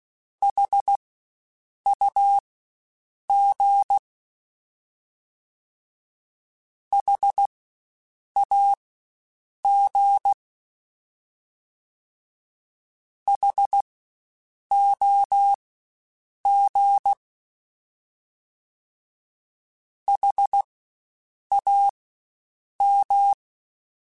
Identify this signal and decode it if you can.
Morse code, heard as a keyed tone, about 16 wpm, spaced as for 7 wpm, 800 Hz. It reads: HUG HAG HOG HAM
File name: Spaced Alphas / Spaced Alphas mp3s / Hs sp @15